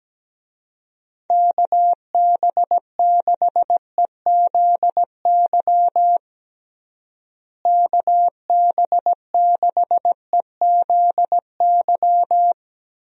Morse code, heard as a keyed tone, about 17 wpm, 700 Hz. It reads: KB6EZY KB6EZY
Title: USA Call Signs - 6 Characters - Repeated 3x - 17wpm